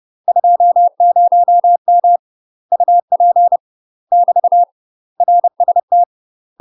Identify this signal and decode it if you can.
Morse code, heard as a keyed tone, about 30 wpm, 700 Hz. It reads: 20M UP <BT> RST
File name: UK MASTER QSOs TRAINING / 30WPM / UK MASTER QSOs 0021